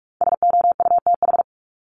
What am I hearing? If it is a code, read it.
Morse code, heard as a keyed tone, about 45 wpm, 700 Hz. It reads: SOUTH
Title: Top 500 Words 45wpm